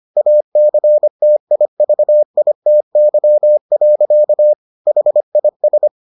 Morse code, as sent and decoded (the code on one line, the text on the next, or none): .- -.-. - .. ...- .. - -.-- .-.-.- / .... .. ...
ACTIVITY. HIS